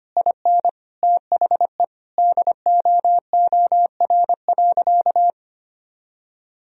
Morse code, heard as a keyed tone, about 25 wpm, 700 Hz. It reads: IN THE DOOR.
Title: Sentences from Top 1000 Words - Review Part 4 - 25wpm